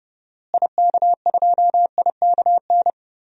Morse code, heard as a keyed tone, about 30 wpm, 700 Hz. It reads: IK2IKN
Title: DXCC-100 Call Signs - 6 Characters - Repeated 3x - 30wpm